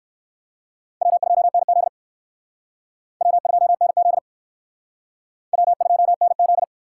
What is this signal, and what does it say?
Morse code, heard as a keyed tone, about 56 wpm, 700 Hz. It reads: W2NZ W2NZ W2NZ
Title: Speed Racing - USA Call Signs 45wpm